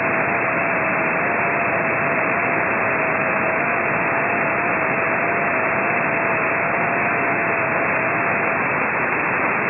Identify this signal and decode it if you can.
FT8.